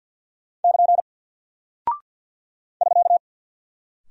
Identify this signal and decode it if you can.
Morse code, heard as a keyed tone, about 48 wpm, 700 Hz. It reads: <KN> <SK>
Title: Single Character (Letter:Number:Punctuation:Prosign) Alt 50wpm